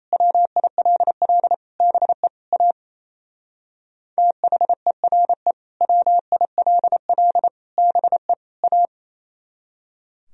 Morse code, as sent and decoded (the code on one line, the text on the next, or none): .-- .. .-.. .-.. / -... / .- / - .... .-. / .-- .. .-.. .-.. / -... / .-
WILL BE A THERE WILL BE A